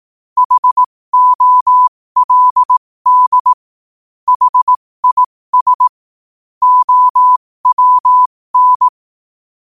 Morse code, heard as a keyed tone, about 18 wpm, spaced as for 15 wpm, 1000 Hz. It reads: HOLD HIS OWN
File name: qotdfiles / 15wpm